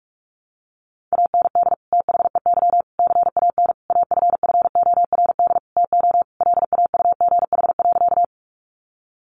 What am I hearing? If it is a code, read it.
Morse code, heard as a keyed tone, about 44 wpm, 700 Hz. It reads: AND THEY CAN AFFORD TO LAUGH.
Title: Sentences from Top 1100 Words 45wpm